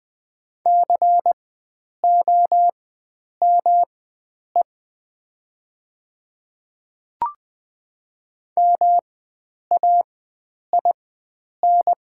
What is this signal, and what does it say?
Morse code, heard as a keyed tone, about 20 wpm, spaced as for 9 wpm, 700 Hz. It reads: COME MAIN
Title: 4x Character Spacing - Top 300 Words - 20wpm